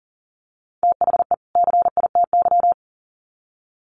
Morse code, heard as a keyed tone, about 40 wpm, 700 Hz. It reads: THE CITY
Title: Sets of 2 Words 40wpm